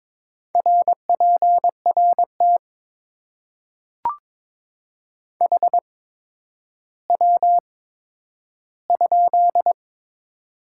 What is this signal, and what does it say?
Morse code, heard as a keyed tone, about 22 wpm, spaced as for 11 wpm, 700 Hz. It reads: RPRT HW?
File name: Top 90 QSO Elements - 8x Character Spacing - 22wpm